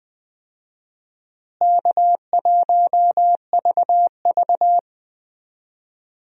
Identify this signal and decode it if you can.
Morse code, heard as a keyed tone, about 20 wpm, 700 Hz. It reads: K1VV
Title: USA Call Signs - 4 Characters - Repeated 3x - 20wpm